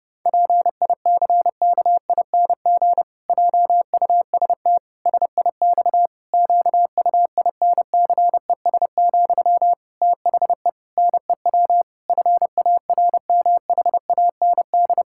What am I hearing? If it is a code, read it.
Morse code, heard as a keyed tone, about 32 wpm, 700 Hz. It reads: PICKING JUST SIX QUINCES, THE NEW FARMHAND
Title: Pangram Challenge v2 - 30wpm